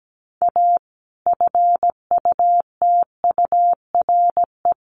Morse code, heard as a keyed tone, about 17 wpm, 700 Hz. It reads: A FUTURE
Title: Sentences from Top 700 Words 17wpm